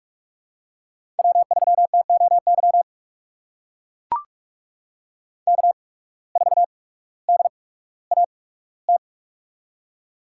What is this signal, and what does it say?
Morse code, heard as a keyed tone, about 44 wpm, spaced as for 19 wpm, 700 Hz. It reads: W3TOY K4DAT
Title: USA Call Signs - 5 Characters - 8x Character Spacing - 45wpm